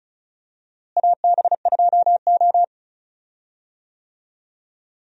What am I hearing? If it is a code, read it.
Morse code, heard as a keyed tone, about 34 wpm, 700 Hz. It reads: AB2O
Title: USA Call Signs - 4 Characters - Repeated 3x - 35wpm